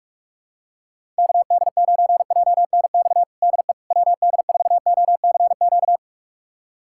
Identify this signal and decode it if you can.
Morse code, heard as a keyed tone, about 43 wpm, 700 Hz. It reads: KD9JNX DE WD4OCQ
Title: USA Callsign DE Callsign 45wpm